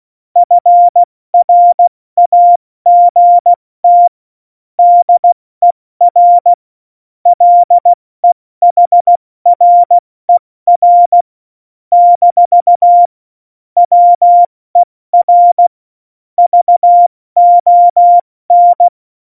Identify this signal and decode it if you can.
Morse code, heard as a keyed tone, about 16 wpm, 700 Hz. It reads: FRAGT DER LEHRER - WER VON
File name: Morsekurs / Witze / Witz 02. April 2019, 70 F80